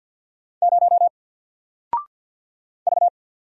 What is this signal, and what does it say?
Morse code, heard as a keyed tone, about 48 wpm, 700 Hz. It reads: 0 V
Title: Single Letter-Number Alt 50wpm